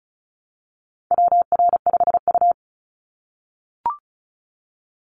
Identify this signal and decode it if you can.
Morse code, heard as a keyed tone, about 36 wpm, 700 Hz. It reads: WR5U
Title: Call Signs - 35wpm